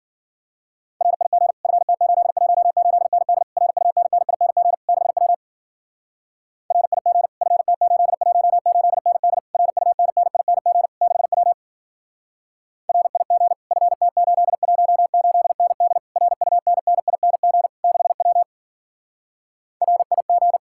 Morse code, heard as a keyed tone, about 50 wpm, 700 Hz. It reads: RIG FT818ND RUNNING 6W RIG FT818ND RUNNING 6W RIG FT818ND RUNNING 6W RIG
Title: Speed-Racing Common QSO Phrases - 40wpm